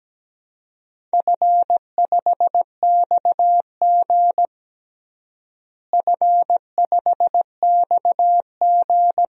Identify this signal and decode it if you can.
Morse code, heard as a keyed tone, about 17 wpm, 700 Hz. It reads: F5XG F5XG
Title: DXCC-100 Call Signs - 4 Characters - Repeated 2x - 17wpm